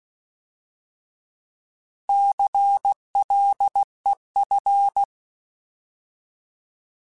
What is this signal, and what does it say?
Morse code, heard as a keyed tone, about 16 wpm, 800 Hz. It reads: CLEF
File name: Alphabetical / Alphabetical mp3s / Cs @15